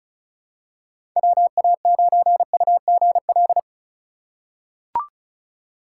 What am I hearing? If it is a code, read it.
Morse code, heard as a keyed tone, about 34 wpm, 700 Hz. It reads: WA9UGL
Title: USA Call Signs - 6 Characters - 4x Character Spacing - 35wpm